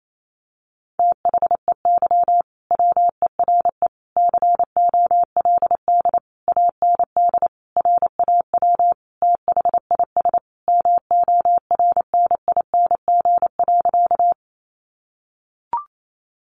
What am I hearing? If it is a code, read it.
Morse code, heard as a keyed tone, about 28 wpm, 700 Hz. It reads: THEY WERE COLD AND RAW THIS MORNING.
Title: Sentences from Top 1500 Words 28wpm